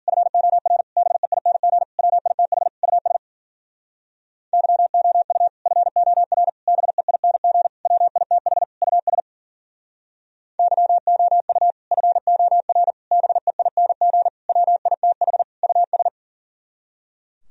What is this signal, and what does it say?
Morse code, heard as a keyed tone, about 47 wpm, 700 Hz. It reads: FOR BEING WITH US YOU FOR BEING WITH US YOU FOR BEING WITH US
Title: Speed-Racing - Sets of 5 Words 40wpm